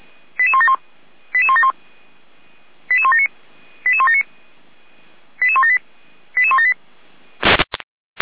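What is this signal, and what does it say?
Five-tone selective call (ZVEI), heard as quick runs of five tones, tones of about 70 ms. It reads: 88171 88171 88179 88179 88178 88178
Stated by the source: ZVEI five-tone call received by SDR.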